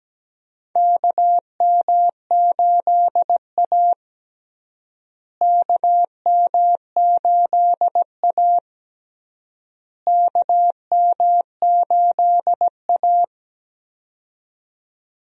Morse code, heard as a keyed tone, about 17 wpm, 700 Hz. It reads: KM8A KM8A KM8A
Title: USA Call Signs - 4 Characters - Repeated 3x - 17wpm